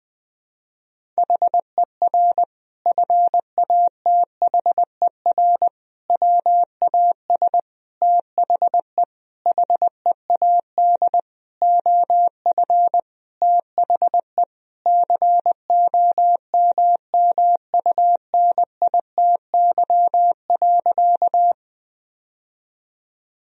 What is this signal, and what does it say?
Morse code, heard as a keyed tone, about 20 wpm, 700 Hz. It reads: HER FATHER WAS THE HEAD OF THE COMMUNITY.
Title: Sentences from Top 1000 Words - Review Part 4 - 20wpm